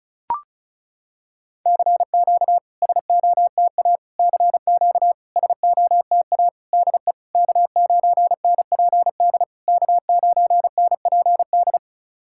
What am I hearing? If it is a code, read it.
Morse code, heard as a keyed tone, about 36 wpm, 690 Hz. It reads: CQ SOTA CQ SOTA DE K9NPD K9NPD
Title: SOTA QSO Practice 35wpm